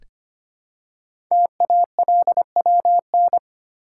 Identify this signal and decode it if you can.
Morse code, heard as a keyed tone, about 25 wpm, 700 Hz. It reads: TALWN
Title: Code Groups - 5 Letters - 25wpm